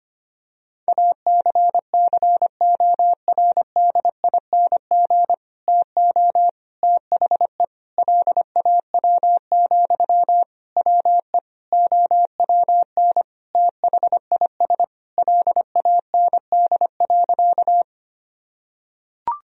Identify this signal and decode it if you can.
Morse code, heard as a keyed tone, about 25 wpm, 700 Hz. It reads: ACCORDING TO THE LAW, WE OWN THIS LAND.
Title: Sentences from Top 1000 Words - Review Part 4 - 25wpm